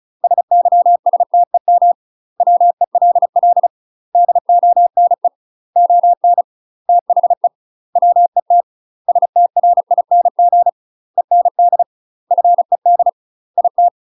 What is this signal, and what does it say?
Morse code, heard as a keyed tone, about 34 wpm, 700 Hz. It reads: SYSTEM WELL DONE ON THE WET STRING END FED IT
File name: UK MASTER QSOs TRAINING / 35WPM / UK MASTER QSOs 0000